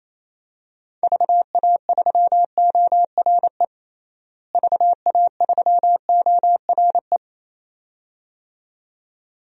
Morse code, heard as a keyed tone, about 28 wpm, 700 Hz. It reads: VA3ORE VA3ORE
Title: DXCC-100 Call Signs - 6 Characters - Repeated 2x - 28wpm